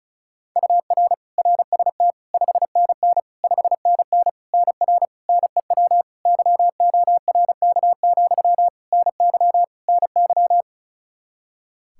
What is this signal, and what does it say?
Morse code, heard as a keyed tone, about 34 wpm, 700 Hz. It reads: UR RST 5NN 5NN NR NEW YORK, NY NY
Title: Practice QSO - Exchange 1 - Beginning - 35wpm